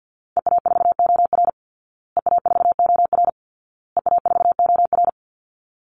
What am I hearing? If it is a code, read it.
Morse code, heard as a keyed tone, about 50 wpm, 700 Hz. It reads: EA4OR EA4OR EA4OR
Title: Call Signs Repeated 2x - 50wpm